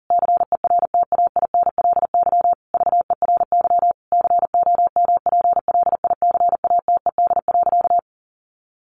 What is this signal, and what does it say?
Morse code, heard as a keyed tone, about 40 wpm, 700 Hz. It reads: CERTAINLY VERY COMPLICATED.
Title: Sentences from Top 1200 Words 40wpm